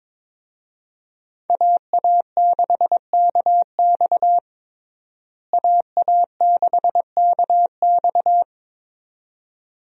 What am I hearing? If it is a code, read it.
Morse code, heard as a keyed tone, about 22 wpm, 700 Hz. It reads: AA6KX AA6KX
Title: USA Call Signs - 5 Characters - Repeated 2x - 22wpm